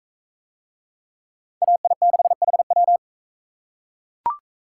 Morse code, heard as a keyed tone, about 43 wpm, 700 Hz. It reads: AI6HW